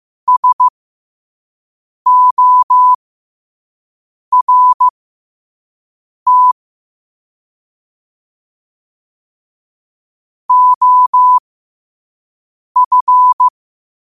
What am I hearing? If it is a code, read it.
Morse code, heard as a keyed tone, about 15 wpm, spaced as for 5 wpm, 1000 Hz. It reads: SORT OF
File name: qotdfiles / 5wpm